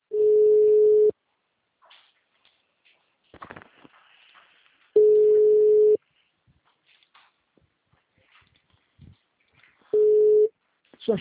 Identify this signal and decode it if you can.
Ringback tone (Europe, 425 Hz), with speech.